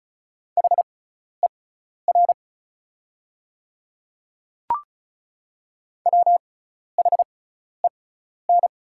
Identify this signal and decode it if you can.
Morse code, heard as a keyed tone, about 34 wpm, spaced as for 12 wpm, 700 Hz. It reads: HER WHEN
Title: 6x Character Spacing - Top 100 Words - 35wpm